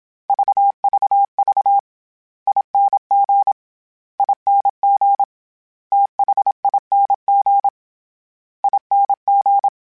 Morse code, heard as a keyed tone, about 27 wpm, 800 Hz. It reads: VVV ING ING THING ING
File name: Special INGs-mp3s / Special INGs-mp3s / SlowToFaster ing @25